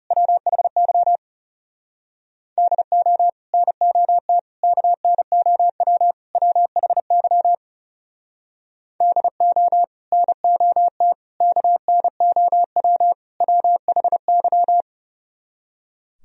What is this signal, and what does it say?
Morse code, heard as a keyed tone, about 40 wpm, 700 Hz. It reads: WHY DO NOT KNOW WHY DO NOT KNOW WHY